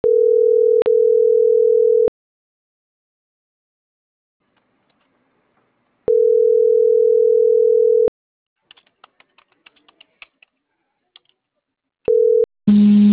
Ringback tone (North America), 440 + 480 Hz.